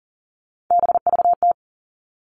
Morse code, heard as a keyed tone, about 40 wpm, 700 Hz. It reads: BVT
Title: All Cut Numbers - 3-Digit 40wpm